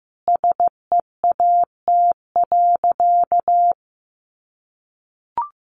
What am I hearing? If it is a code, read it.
Morse code, heard as a keyed tone, about 15 wpm, 700 Hz. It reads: SEAT.
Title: Sentences from Top 1500 Words 15wpm